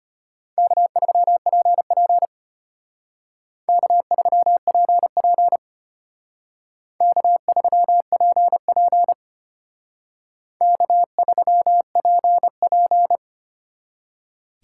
Morse code, heard as a keyed tone, about 32 wpm, 700 Hz. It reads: K3PP K3PP K3PP K3PP